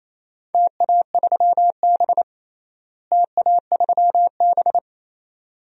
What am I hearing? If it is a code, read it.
Morse code, heard as a keyed tone, about 28 wpm, 700 Hz. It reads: TA3B TA3B
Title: DXCC-100 Call Signs - 4 Characters - Repeated 2x - 28wpm